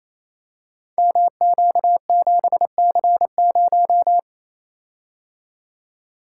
Morse code, heard as a keyed tone, about 29 wpm, 700 Hz. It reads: MQ7C0
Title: Code Groups - 5 Letter:Numbers - 28wpm